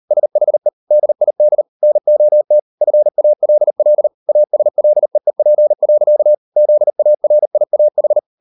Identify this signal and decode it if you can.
Morse code, heard as a keyed tone, about 38 wpm, 600 Hz. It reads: SHE DID NOT FALL ASLEEP. ZARIAH